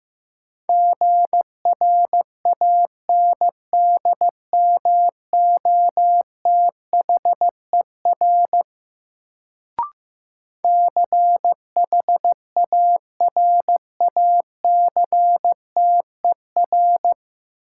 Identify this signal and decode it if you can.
Morse code, heard as a keyed tone, about 15 wpm, 700 Hz. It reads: GRANDMOTHER CHARACTER
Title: Sentences from Top 1200 Words - New Word Drill Alt - 15wpm